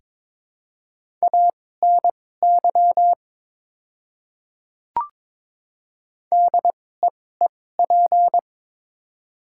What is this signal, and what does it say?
Morse code, heard as a keyed tone, about 21 wpm, spaced as for 16 wpm, 700 Hz. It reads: ANY DEEP
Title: 2x Character Spacing - Top 500 Words - 22wpm